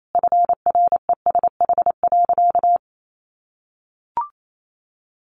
Morse code, heard as a keyed tone, about 28 wpm, 700 Hz. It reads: FRESH.
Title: Sentences from Top 1200 Words 28wpm